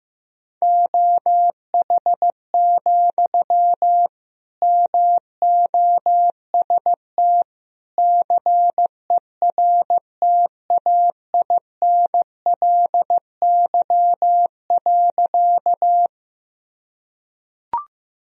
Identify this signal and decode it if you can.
Morse code, heard as a keyed tone, about 15 wpm, 700 Hz. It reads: OH, MOST CERTAINLY.
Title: Sentences from Top 1000 Words - Review Part 1 - 15wpm